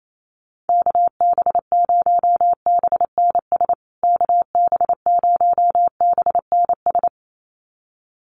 Morse code, heard as a keyed tone, about 28 wpm, 700 Hz. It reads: KB0BNS KB0BNS
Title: USA Call Signs Repeated 2x 28wpm